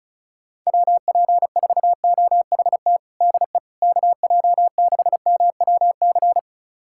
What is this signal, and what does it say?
Morse code, heard as a keyed tone, about 34 wpm, 700 Hz. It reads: WP4OHT DE KJ6MWC
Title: USA Callsign DE Callsign 35wpm